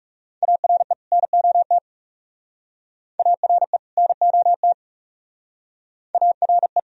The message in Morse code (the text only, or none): ARE NOT ARE NOT ARE